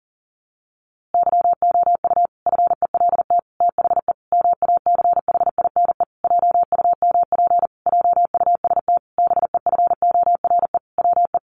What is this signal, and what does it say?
Morse code, heard as a keyed tone, about 40 wpm, 700 Hz. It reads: YOU FELT THE MACHINE JUMP JUST BEFORE WE
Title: A Sound Of Thunder 40wpm